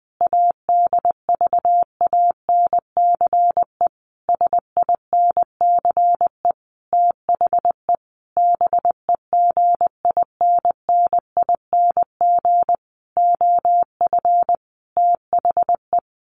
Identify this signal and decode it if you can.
Morse code, heard as a keyed tone, about 20 wpm, 700 Hz. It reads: ADVANCE SINCE THE BEGINNING OF THE
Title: Sentences from Top 1400 Words 20wpm